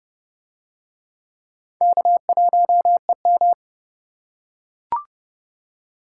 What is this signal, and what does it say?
Morse code, heard as a keyed tone, about 30 wpm, 700 Hz. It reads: K1EM